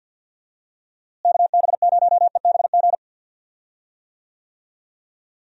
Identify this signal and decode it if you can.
Morse code, heard as a keyed tone, about 48 wpm, 700 Hz. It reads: KB0EBG
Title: USA Call Signs - 6 Characters - Repeated 2x - 50wpm